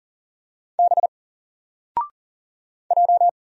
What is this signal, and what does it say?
Morse code, heard as a keyed tone, about 38 wpm, 700 Hz. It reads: B J
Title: Single Letter-Number 40wpm